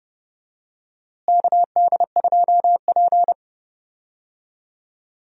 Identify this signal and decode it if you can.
Morse code, heard as a keyed tone, about 32 wpm, 700 Hz. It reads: KD2P
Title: USA Call Signs - 4 Characters - Repeated 3x - 30wpm